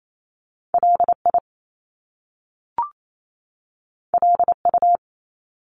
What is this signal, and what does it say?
Morse code, heard as a keyed tone, about 28 wpm, 700 Hz. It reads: LI LU
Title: Top Two Letter Combinations in Words 28wpm